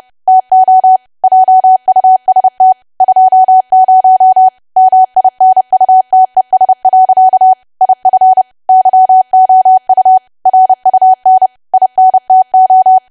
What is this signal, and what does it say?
Morse code, heard as a keyed tone, about 30 wpm, 750 Hz. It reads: TO JUST 20 MINUTES. IF YOU RUN INTO
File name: cwa / W1AW-Level3 / 30WPM